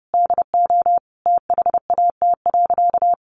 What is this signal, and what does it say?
Morse code, heard as a keyed tone, about 30 wpm, 700 Hz. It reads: DO THAT.
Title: Sentences from Top 800 Words 30wpm